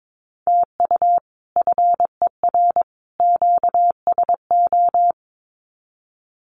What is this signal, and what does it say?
Morse code, heard as a keyed tone, about 22 wpm, 700 Hz. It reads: TU FER QSO
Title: Common QSO Phrases 22wpm